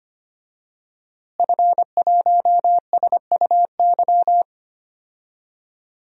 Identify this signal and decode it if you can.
Morse code, heard as a keyed tone, about 25 wpm, 700 Hz. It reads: F1SUY
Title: DXCC-100 Call Signs - 5 Characters - Repeated 2x - 25wpm